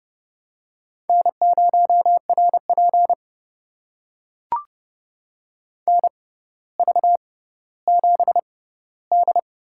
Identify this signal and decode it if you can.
Morse code, heard as a keyed tone, about 30 wpm, spaced as for 15 wpm, 700 Hz. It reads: N0RP NV7D
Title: USA Call Signs - 4 Characters - 6x Character Spacing - 30wpm